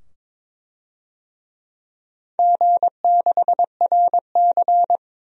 Morse code, heard as a keyed tone, about 22 wpm, 700 Hz. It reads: G6RC
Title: DXCC-100 Call Signs - 4 Characters - 22wpm